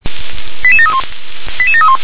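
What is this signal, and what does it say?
Five-tone selective call (ZVEI), heard as quick runs of five tones, tones of about 70 ms. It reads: 88731 88731
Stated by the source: ZVEI five-tone call 88951 received by SDR.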